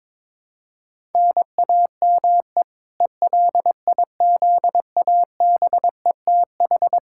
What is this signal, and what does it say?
Morse code, heard as a keyed tone, about 22 wpm, 700 Hz. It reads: NAME ELIZABETH
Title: K1USN SST QSO Practice 22wpm